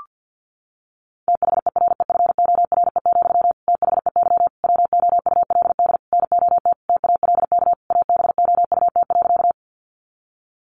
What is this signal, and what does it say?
Morse code, heard as a keyed tone, about 50 wpm, 700 Hz. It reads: THEREFORE, THEY WOULD NOT TALK ABOUT.
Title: Sentences from Top 800 Words 50wpm